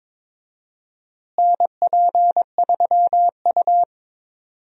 Morse code, heard as a keyed tone, about 22 wpm, 700 Hz. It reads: NP3U
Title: USA Call Signs - 4 Characters - Repeated 2x - 22wpm